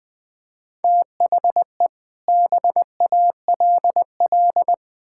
Morse code, heard as a keyed tone, about 20 wpm, 700 Hz. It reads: THE BALL
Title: Sets of 8 Words - 20wpm